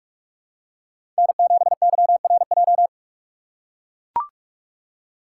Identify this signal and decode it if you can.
Morse code, heard as a keyed tone, about 41 wpm, 700 Hz. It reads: N7YRJ
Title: USA Call Signs - 5 Characters - 4x Character Spacing - 45wpm